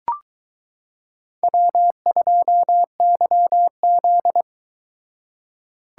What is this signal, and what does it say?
Morse code, heard as a keyed tone, about 22 wpm, 700 Hz. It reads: W2YZ